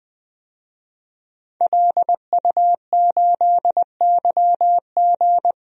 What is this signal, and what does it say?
Morse code, heard as a keyed tone, about 20 wpm, 700 Hz. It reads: LU8YG
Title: DXCC-100 Call Signs - 5 Characters - Repeated 3x - 20wpm